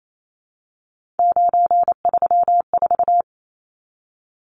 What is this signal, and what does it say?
Morse code, heard as a keyed tone, about 28 wpm, 700 Hz. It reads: 934
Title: Numbers - 3-Digit 28wpm